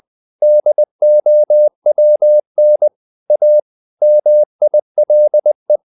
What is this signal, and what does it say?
Morse code, heard as a keyed tone, about 20 wpm, 600 Hz. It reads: DOWN A MILE
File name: the old man and the sea part 2 20wpm